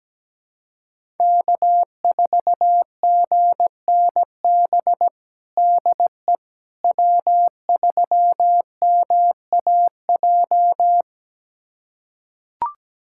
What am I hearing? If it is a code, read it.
Morse code, heard as a keyed tone, about 17 wpm, 700 Hz. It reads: K4GNB DE W3MAJ
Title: USA Callsign DE Callsign - 5 Characters - 17wpm